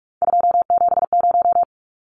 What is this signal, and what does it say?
Morse code, heard as a keyed tone, about 44 wpm, 700 Hz. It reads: 270
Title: Numbers - 3-Digit 45wpm